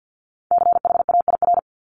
Morse code, heard as a keyed tone, about 50 wpm, 700 Hz. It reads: CHAIR
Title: Top 1751-2000 Words 50wpm